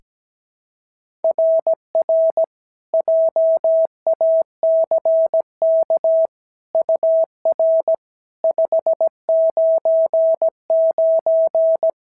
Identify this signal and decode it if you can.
Morse code, heard as a keyed tone, about 16 wpm, 650 Hz. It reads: RR JACK UR 599
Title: SOTA QSO Practice 17wpm